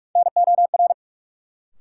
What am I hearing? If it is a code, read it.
Morse code, heard as a keyed tone, about 41 wpm, 700 Hz. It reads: NOR
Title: Sentences from Top 1500 Words - New Word Drill - 45wpm